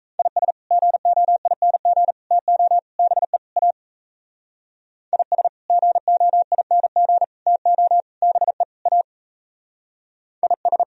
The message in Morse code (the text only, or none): IS GOING TO BE A IS GOING TO BE A IS